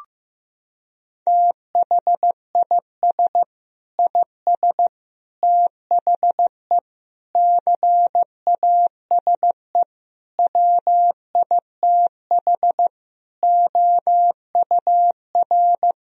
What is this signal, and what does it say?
Morse code, heard as a keyed tone, about 15 wpm, 700 Hz. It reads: THIS IS THE CASE WITH OUR
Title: Sentences from Top 1000 Words - Review Part 3 - 15wpm